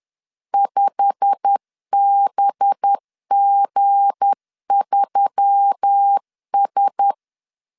Morse code, heard as a keyed tone, about 10 wpm, 800 Hz. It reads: 5BG3S
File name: LEZIONI / 9 28 10wpm 2842